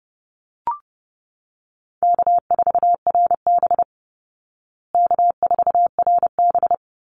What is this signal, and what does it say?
Morse code, heard as a keyed tone, about 30 wpm, 700 Hz. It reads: K4RB K4RB
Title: Easy Call Signs Repeated 3x - 30wpm